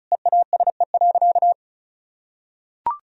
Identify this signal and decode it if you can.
Morse code, heard as a keyed tone, about 36 wpm, 700 Hz. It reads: EASE.
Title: Sentences from Top 1000 Words - Review Part 1 - 35wpm